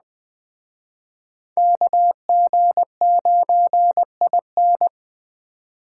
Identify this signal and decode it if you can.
Morse code, heard as a keyed tone, about 20 wpm, 700 Hz. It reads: KG9IN